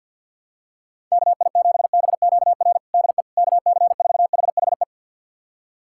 Morse code, heard as a keyed tone, about 47 wpm, 700 Hz. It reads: KI7BQR DE KC4HHE
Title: USA Callsign DE Callsign 50wpm